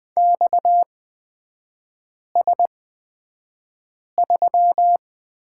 Morse code, heard as a keyed tone, about 20 wpm, spaced as for 5 wpm, 700 Hz. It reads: XS3
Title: 250 Lesson 40e - BK - Three Characters 20wpm